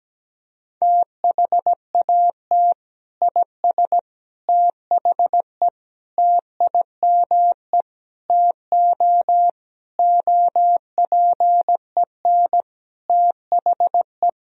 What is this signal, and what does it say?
Morse code, heard as a keyed tone, about 17 wpm, 700 Hz. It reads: THAT IS THE TIME TO OPEN THE
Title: Sentences from Top 1000 Words - Review Part 1 - 17wpm